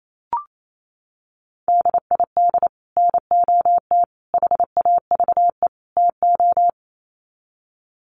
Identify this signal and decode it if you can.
Morse code, heard as a keyed tone, about 28 wpm, 700 Hz. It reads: DID NOT HAVE TO
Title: Sets of 4 Words 28wpm